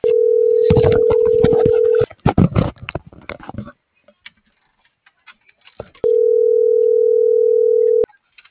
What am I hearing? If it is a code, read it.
Ringback tone (North America), 440 + 480 Hz.